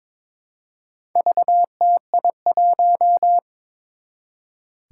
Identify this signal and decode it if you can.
Morse code, heard as a keyed tone, about 22 wpm, 700 Hz. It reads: VTI1
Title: Four Challenging Letter-Number Combinations 22wpm